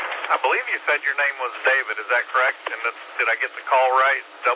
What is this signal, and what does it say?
Emergency Alert System (SAME) tones, heard as harsh buzzing bursts, with speech. It is heard from 0.6 to 3.6 s.